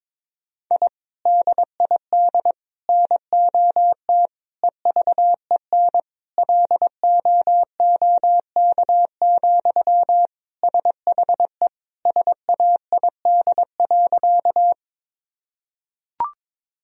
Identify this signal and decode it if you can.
Morse code, heard as a keyed tone, about 21 wpm, 700 Hz. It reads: I DID NOT EVEN LOOK, SHE SAID.
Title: Sentences from Top 1000 Words - Review Part 1 - 22wpm